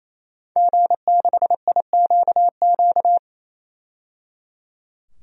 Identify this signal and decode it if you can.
Morse code, heard as a keyed tone, about 28 wpm, 700 Hz. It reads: G6IQQ
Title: DXCC-100 Call Signs - 5 Characters - 28wpm